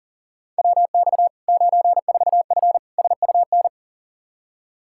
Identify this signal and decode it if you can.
Morse code, heard as a keyed tone, about 40 wpm, 700 Hz. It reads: WX 94F SUN
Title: Practice QSO - Exchange 2 - Info - 40wpm